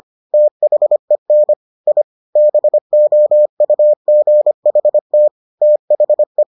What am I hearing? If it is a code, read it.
Morse code, heard as a keyed tone, about 25 wpm, 600 Hz. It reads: THEN I BOUGHT THE